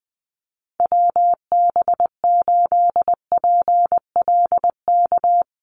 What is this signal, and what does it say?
Morse code, heard as a keyed tone, about 19 wpm, 700 Hz. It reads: WB8PLK
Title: USA Call Signs Repeated 2x 20wpm